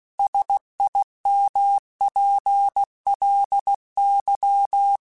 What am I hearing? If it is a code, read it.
Morse code, heard as a keyed tone, about 16 wpm, 800 Hz. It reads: SIMPLY
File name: Technical Subjects / Technical Subjects-mp3s / Sw Pwr Sup @15